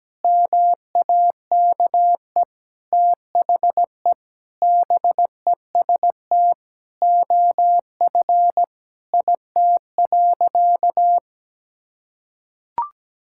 Morse code, heard as a keyed tone, about 17 wpm, 700 Hz. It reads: MAKE THE BEST OF IT.
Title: Sentences from Top 1000 Words - Review Part 5 - 17wpm